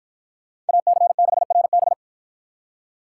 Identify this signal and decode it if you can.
Morse code, heard as a keyed tone, about 47 wpm, 700 Hz. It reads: AC6RB